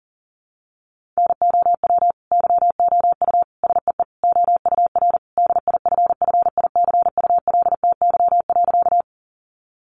Morse code, heard as a keyed tone, about 40 wpm, 700 Hz. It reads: NOW YOU SEE OUR DIFFICULTY.
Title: Sentences from Top 1500 Words 40wpm